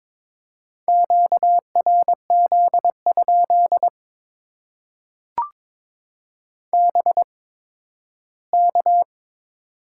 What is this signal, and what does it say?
Morse code, heard as a keyed tone, about 22 wpm, spaced as for 13 wpm, 700 Hz. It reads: QRZ? BK
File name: Top 90 QSO Elements - 8x Character Spacing - 22wpm